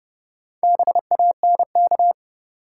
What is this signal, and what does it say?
Morse code, heard as a keyed tone, about 30 wpm, 700 Hz. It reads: BANK
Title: Sets of 5 Words - New Word Drill - 30wpm